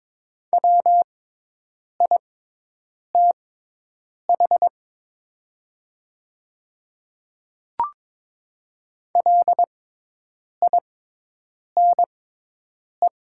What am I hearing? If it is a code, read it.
Morse code, heard as a keyed tone, about 22 wpm, spaced as for 8 wpm, 700 Hz. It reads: WITH LINE
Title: 6x Character Spacing - Top 500 Words - 22wpm